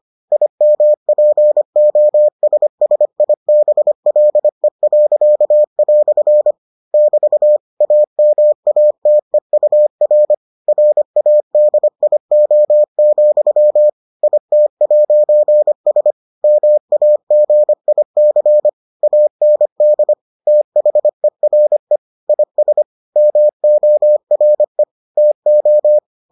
Morse code, heard as a keyed tone, about 25 wpm, 600 Hz. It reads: IMPOSSIBLE." <BT> AMATEUR RADIO, IT'S MAGIC AND THERE IS MORE TO